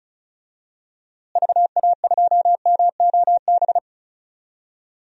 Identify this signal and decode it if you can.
Morse code, heard as a keyed tone, about 36 wpm, 700 Hz. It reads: VA2MOB
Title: DXCC-100 Call Signs - 6 Characters - Repeated 3x - 35wpm